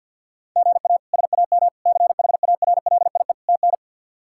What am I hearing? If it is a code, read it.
Morse code, heard as a keyed tone, about 48 wpm, 700 Hz. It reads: GA SAM CHARLIE TN
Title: K1USN SST QSO Practice - 50wpm